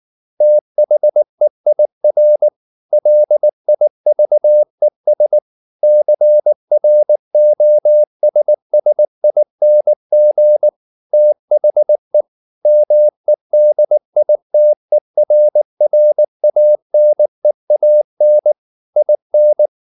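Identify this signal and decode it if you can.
Morse code, heard as a keyed tone, about 19 wpm, 600 Hz. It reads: THEIR LIVES CROSSING THE MEDITERRANEAN IN
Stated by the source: Morse code at 19 wpm from CW med Gnister, reading Ken Follett's novel Never in English.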